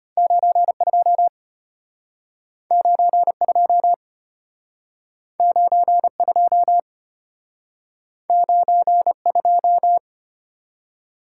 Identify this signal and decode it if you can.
Morse code, heard as a keyed tone, about 34 wpm, 700 Hz. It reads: 92 92 92 92